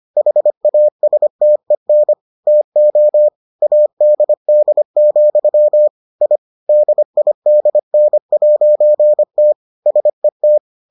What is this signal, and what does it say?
Morse code, heard as a keyed tone, about 25 wpm, 600 Hz. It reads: HASTEN TO ADD, I DIDN'T SET